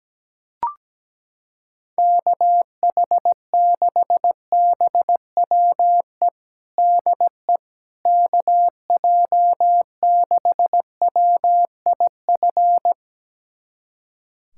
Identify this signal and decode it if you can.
Morse code, heard as a keyed tone, about 17 wpm, 700 Hz. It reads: KH6BWE DE KJ6WIF
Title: USA Callsign DE Callsign 17wpm